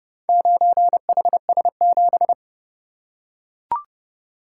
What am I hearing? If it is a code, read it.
Morse code, heard as a keyed tone, about 29 wpm, 700 Hz. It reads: 9HS7